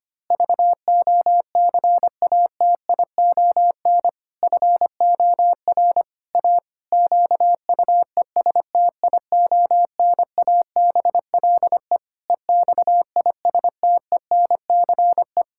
Morse code, heard as a keyed tone, about 25 wpm, 700 Hz. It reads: VOCATION FOR A QUESTIONABLE EXISTENCE
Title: Pangram Challenge v2 - 25wpm